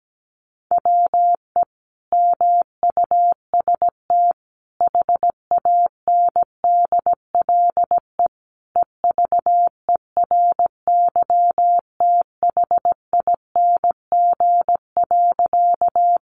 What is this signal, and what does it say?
Morse code, heard as a keyed tone, about 17 wpm, 700 Hz. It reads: WE MUST HANDLE EVERYTHING.
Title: Sentences from Top 1100 Words 17wpm